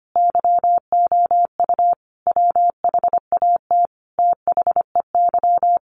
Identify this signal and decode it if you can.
Morse code, heard as a keyed tone, about 25 wpm, 700 Hz. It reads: YOU WHAT THEY
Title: Sentences from Top 1000 Words 25wpm